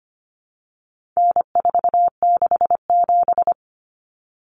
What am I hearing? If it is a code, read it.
Morse code, heard as a keyed tone, about 25 wpm, 700 Hz. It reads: N467
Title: Common Cut Numbers - 4-Digit 25wpm